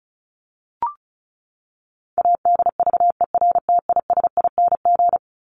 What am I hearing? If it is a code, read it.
Morse code, heard as a keyed tone, about 34 wpm, 700 Hz. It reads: ADVERTISING
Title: Top 751-1000 Words 35wpm